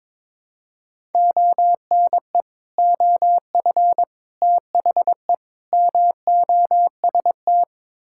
Morse code, heard as a keyed tone, about 21 wpm, 700 Hz. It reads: ONE OF THE MOST
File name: practice / 22wpm / Build A Phrase - 5-Word Phrases - 22wpm